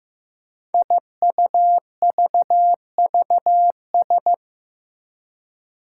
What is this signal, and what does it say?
Morse code, heard as a keyed tone, about 15 wpm, 700 Hz. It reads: IUVVS
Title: Five Challenging Letter-Number Combinations 15wpm